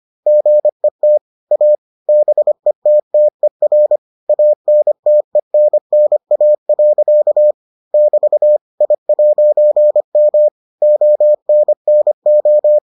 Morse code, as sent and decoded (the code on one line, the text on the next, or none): --. - / .- / -... - - . .-. / .- -. - . -. -. .- .-.-.- / -...- / .. .----. -- / --- -. -. ---
GET A BETTER ANTENNA. <BT> I'M ONNO